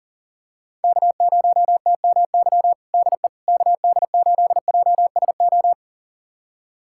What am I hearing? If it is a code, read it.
Morse code, heard as a keyed tone, about 40 wpm, 700 Hz. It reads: K0TMY DE KD8JSO
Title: USA Callsign DE Callsign 40wpm